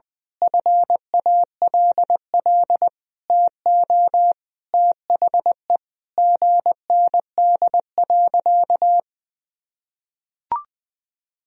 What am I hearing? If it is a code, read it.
Morse code, heard as a keyed tone, about 20 wpm, 700 Hz. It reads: FALL TO THE GND.
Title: Sentences from Top 500 Words and Abbreviations 20wpm